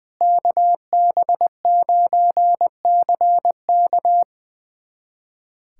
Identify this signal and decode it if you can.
Morse code, heard as a keyed tone, about 21 wpm, 700 Hz. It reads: KB9CK
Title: USA Callsign DE Callsign - 5 Characters - 20wpm